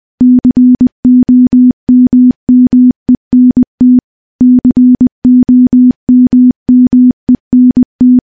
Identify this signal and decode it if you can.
Morse code, heard as a keyed tone, about 21 wpm, 250 Hz. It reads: COMMENT COMMENT
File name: morse-training-20WPM